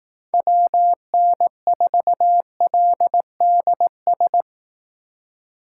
Morse code, heard as a keyed tone, about 18 wpm, 700 Hz. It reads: WN4LDS
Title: Speed Racing - USA Call Signs 15wpm